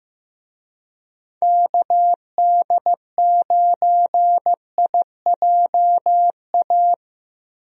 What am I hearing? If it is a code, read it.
Morse code, heard as a keyed tone, about 15 wpm, 700 Hz. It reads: KD9IJA
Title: USA Call Signs - 6 Characters - Repeated 3x - 15wpm